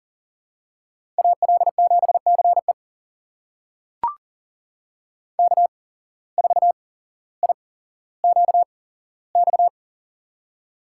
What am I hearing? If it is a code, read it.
Morse code, heard as a keyed tone, about 40 wpm, spaced as for 17 wpm, 700 Hz. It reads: AL7CE K4IQX
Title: USA Call Signs - 5 Characters - 8x Character Spacing - 40wpm